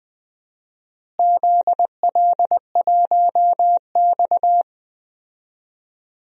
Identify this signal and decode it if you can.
Morse code, heard as a keyed tone, about 20 wpm, 700 Hz. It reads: ZL1X
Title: DXCC-100 Call Signs - 4 Characters - Repeated 2x - 20wpm